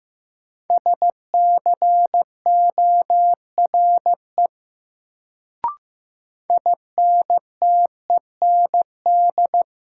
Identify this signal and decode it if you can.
Morse code, heard as a keyed tone, about 15 wpm, 700 Hz. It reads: SCORE INTEND
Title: Sentences from Top 1600 Words - New Word Drill Alt - 15wpm